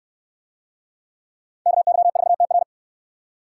Morse code, heard as a keyed tone, about 69 wpm, 700 Hz. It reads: KY4TR